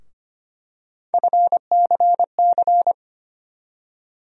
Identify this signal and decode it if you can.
Morse code, heard as a keyed tone, about 25 wpm, 700 Hz. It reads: FCC
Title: USA Call Sign Postfixes - 3 Letters 25wpm